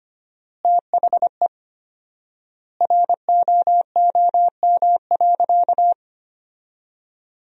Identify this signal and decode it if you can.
Morse code, heard as a keyed tone, about 25 wpm, 700 Hz. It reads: THE ROOM.
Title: 4x Word-Spacing - Sentences from Top 300 Words - 25wpm